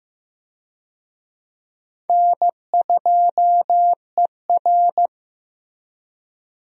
Morse code, heard as a keyed tone, about 15 wpm, 700 Hz. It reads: N2ER
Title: USA Call Signs - 4 Characters - Repeated 3x - 15wpm